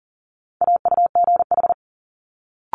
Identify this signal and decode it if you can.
Morse code, heard as a keyed tone, about 40 wpm, 700 Hz. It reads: AUGH
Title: Multi-letter Phonemes Quick Fire 40wpm